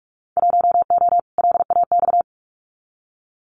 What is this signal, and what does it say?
Morse code, heard as a keyed tone, about 46 wpm, 700 Hz. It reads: 1O LAX
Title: Winter Field Day Exchanges 45wpm